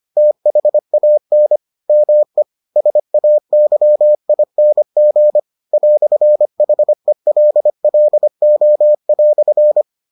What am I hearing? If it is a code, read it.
Morse code, heard as a keyed tone, about 25 wpm, 600 Hz. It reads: THAN ME SAYING "HELLO"